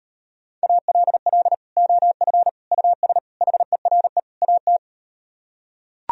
Morse code, heard as a keyed tone, about 38 wpm, 700 Hz. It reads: ALL OF US HERE AT